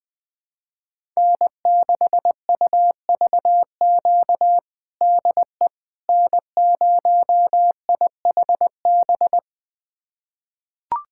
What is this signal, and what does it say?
Morse code, heard as a keyed tone, about 21 wpm, 700 Hz. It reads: N6UVQ DE N0IHB
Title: USA Callsign DE Callsign - 5 Characters - 20wpm